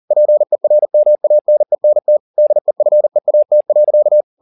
Morse code, heard as a keyed tone, about 40 wpm, 600 Hz. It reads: PERMANENT DEFEAT.